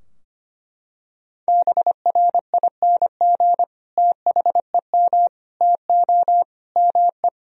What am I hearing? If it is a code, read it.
Morse code, heard as a keyed tone, about 25 wpm, 700 Hz. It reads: BRING THEM TO ME
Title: Sentences from Top 1000 Words - Review Part 3 - 25wpm